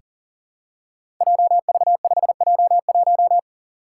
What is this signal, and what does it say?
Morse code, heard as a keyed tone, about 41 wpm, 700 Hz. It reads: JV5J1
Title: Five Challenging Letter-Number Combinations 40wpm